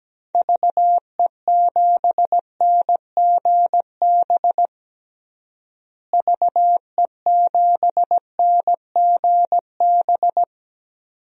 Morse code, heard as a keyed tone, about 17 wpm, 700 Hz. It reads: VE7NGB VE7NGB
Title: DXCC-100 Call Signs - 6 Characters - Repeated 2x - 17wpm